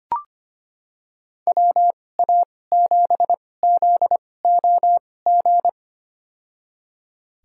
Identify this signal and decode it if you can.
Morse code, heard as a keyed tone, about 25 wpm, spaced as for 18 wpm, 700 Hz. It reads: WA7ZOG
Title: USA Call Signs - 6 Characters - 2x Character Spacing - 25wpm